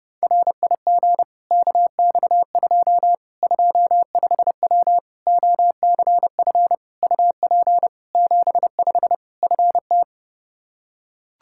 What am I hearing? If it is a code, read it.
Morse code, heard as a keyed tone, about 30 wpm, 700 Hz. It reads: RIG KX2 25W OCF UP 75 FT
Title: K1USN SST QSO Practice 30wpm